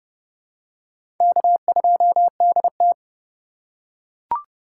Morse code, heard as a keyed tone, about 30 wpm, 700 Hz. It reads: K2DT
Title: USA Call Signs - 4 Characters - 2x Character Spacing - 30wpm